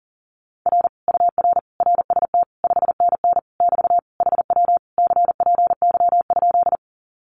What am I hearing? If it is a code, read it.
Morse code, heard as a keyed tone, about 37 wpm, 700 Hz. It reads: R UR RST 5NN <BT> HW CPY?
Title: Common QSO Phrases Encore 40wpm